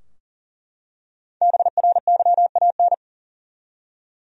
Morse code, heard as a keyed tone, about 40 wpm, 700 Hz. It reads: BRYAN
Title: USA Common Names - Max 5 Letters - 40wpm